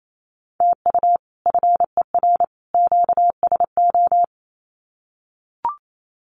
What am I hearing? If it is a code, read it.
Morse code, heard as a keyed tone, about 28 wpm, 700 Hz. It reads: TU FER QSO
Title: Common QSO Phrases 28wpm